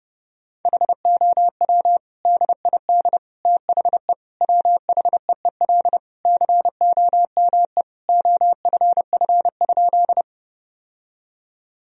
Morse code, heard as a keyed tone, about 31 wpm, 700 Hz. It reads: HOW DID THE WHEEL COME OFF?
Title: Sentences from Top 1000 Words - Review Part 3 - 30wpm